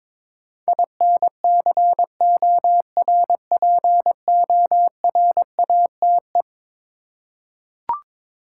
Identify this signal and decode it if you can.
Morse code, heard as a keyed tone, about 22 wpm, 700 Hz. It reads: INCORPORATE
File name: practice / 22wpm / Eleven Letter Words - 22wpm